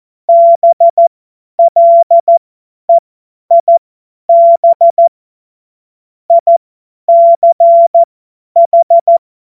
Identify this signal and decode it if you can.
Morse code, heard as a keyed tone, about 14 wpm, spaced as for 10 wpm, 680 Hz. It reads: BLEIB ICH
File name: Morsekurs / Witze / Witz 04. April 2023 50 F70